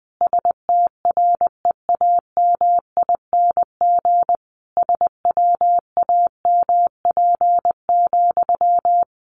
Morse code, heard as a keyed tone, about 20 wpm, 700 Hz. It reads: STREAMING SWAMP,
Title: A Sound Of Thunder 20wpm